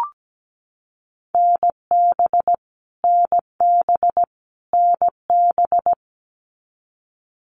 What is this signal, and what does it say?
Morse code, heard as a keyed tone, about 17 wpm, 700 Hz. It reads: NB NB NB
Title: Canadian Provinces and Territories - Repeated 3x - 17wpm